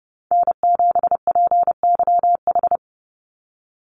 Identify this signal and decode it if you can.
Morse code, heard as a keyed tone, about 30 wpm, 700 Hz. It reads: N7PYH
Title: USA Call Signs 30wpm